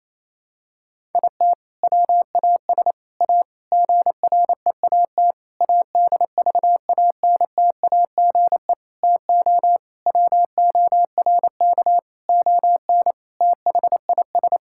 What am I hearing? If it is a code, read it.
Morse code, heard as a keyed tone, about 28 wpm, 700 Hz. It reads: IT WAS A GREAT ADVANTAGE TO WORK ON THIS
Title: Sentences from Top 1000 Words - Review Part 6 - 28wpm